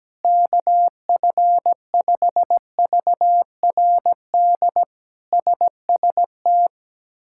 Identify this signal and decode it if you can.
Morse code, heard as a keyed tone, about 17 wpm, 700 Hz. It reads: KF5VRD SST
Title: K1USN SST QSO Practice - 17wpm